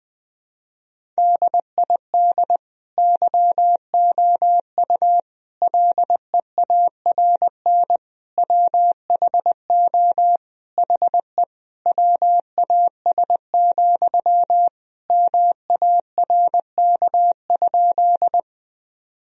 Morse code, heard as a keyed tone, about 20 wpm, 700 Hz. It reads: DID YOU LEARN WHO HE WAS, MARK?
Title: Sentences from Top 1000 Words - Review Part 1 - 20wpm